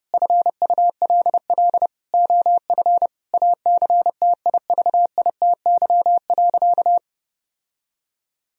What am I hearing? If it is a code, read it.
Morse code, heard as a keyed tone, about 30 wpm, 700 Hz. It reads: FULL OF ACTIVITY.
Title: Sentences from Top 1000 Words - Review Part 4 - 30wpm